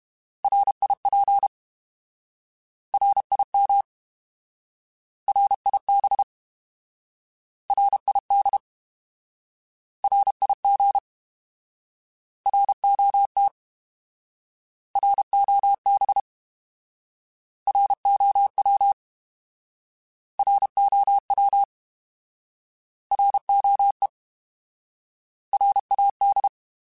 Morse code, heard as a keyed tone, about 32 wpm, 800 Hz. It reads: RIP RIM RIB RID RIG ROT ROB ROW ROW ROE RAD
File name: Alphabetical / Alphabetical mp3s / Rs @30